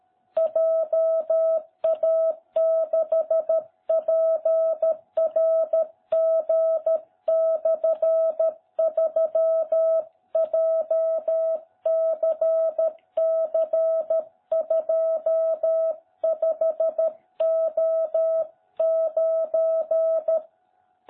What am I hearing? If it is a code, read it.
Morse code, heard as a keyed tone, about 13 wpm, 660 Hz. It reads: JA6PRG/3 JCC25O9